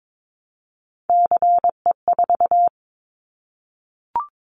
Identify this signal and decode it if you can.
Morse code, heard as a keyed tone, about 22 wpm, 700 Hz. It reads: CE4
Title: Callsign Prefixes for Top 100 DXCC - 22wpm